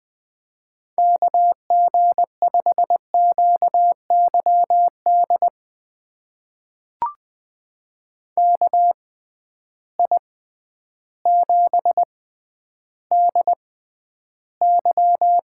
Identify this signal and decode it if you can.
Morse code, heard as a keyed tone, about 20 wpm, spaced as for 11 wpm, 700 Hz. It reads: KG5QYD KI7DY
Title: USA Call Signs - 6 Characters - 6x Character Spacing - 20wpm